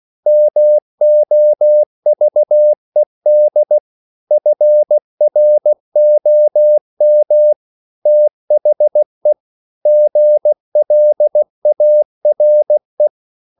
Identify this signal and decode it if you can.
Morse code, heard as a keyed tone, about 16 wpm, 600 Hz. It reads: MOVED FROM THE GLARE